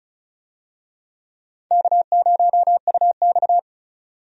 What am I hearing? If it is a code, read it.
Morse code, heard as a keyed tone, about 34 wpm, 700 Hz. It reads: K0UX
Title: USA Call Signs - 4 Characters - 35wpm